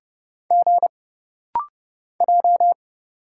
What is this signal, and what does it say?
Morse code, heard as a keyed tone, about 30 wpm, 700 Hz. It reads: G J
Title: Single Letters - Rapid-Fire 30wpm